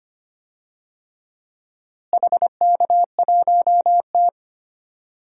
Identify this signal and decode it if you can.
Morse code, heard as a keyed tone, about 25 wpm, 700 Hz. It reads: HK1T
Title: DXCC-100 Call Signs - 4 Characters - Repeated 3x - 25wpm